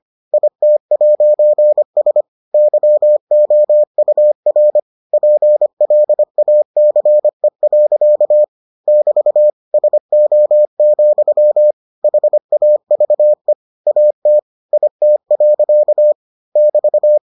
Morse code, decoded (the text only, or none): IT'S YOUR PLACE. <BT> SO, HAVE AT IT. <BT>